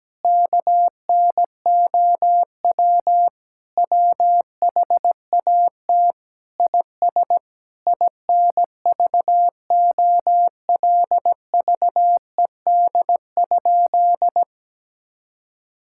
Morse code, heard as a keyed tone, about 17 wpm, 700 Hz. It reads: KNOW WHAT IS INVOLVED?
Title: Sentences from Top 1000 Words - Review Part 5 - 17wpm